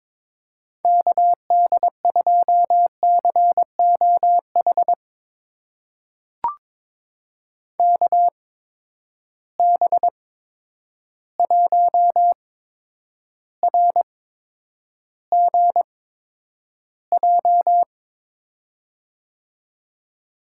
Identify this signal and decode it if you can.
Morse code, heard as a keyed tone, about 22 wpm, spaced as for 9 wpm, 700 Hz. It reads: KD2COH KB1RGJ
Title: USA Call Signs - 6 Characters - 8x Character Spacing - 22wpm